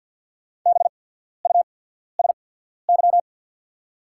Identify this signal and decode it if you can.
Morse code, heard as a keyed tone, about 48 wpm, spaced as for 14 wpm, 700 Hz. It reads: BUSY
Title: 8x Character Spacing - Top 500 Words - 50wpm